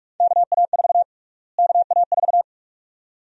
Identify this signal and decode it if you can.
Morse code, heard as a keyed tone, about 47 wpm, 700 Hz. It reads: KA4 KA4
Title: USA Call Sign Prefixes - Repeated 2x - 45wpm